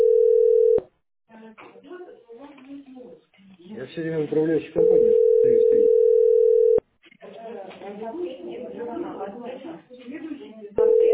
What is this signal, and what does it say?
Ringback tone (North America), 440 + 480 Hz, with speech.
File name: ce6f425e-19e6-43f5-a331